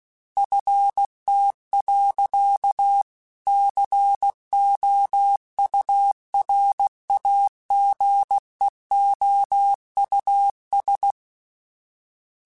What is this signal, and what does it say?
Morse code, heard as a keyed tone, about 16 wpm, 800 Hz. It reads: FT. COURAGEOUS
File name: fromSMs / FromSMs mp3s / Cities@15